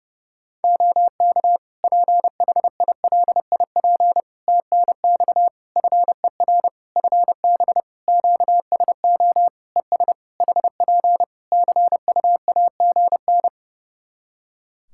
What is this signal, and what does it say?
Morse code, heard as a keyed tone, about 30 wpm, 700 Hz. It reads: OK PHILIP TNX FER FB QSO ES HP CUAGN
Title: K1USN SST QSO Practice 30wpm